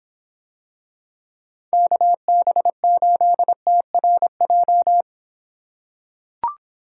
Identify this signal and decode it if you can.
Morse code, heard as a keyed tone, about 26 wpm, 700 Hz. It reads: KB8TRJ